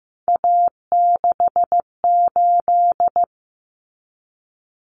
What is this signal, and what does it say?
Morse code, heard as a keyed tone, about 15 wpm, 700 Hz. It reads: A68
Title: Common Cut Numbers - 3-Digit 15wpm